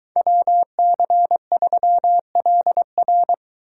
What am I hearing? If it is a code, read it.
Morse code, heard as a keyed tone, about 23 wpm, 700 Hz. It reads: WC3LR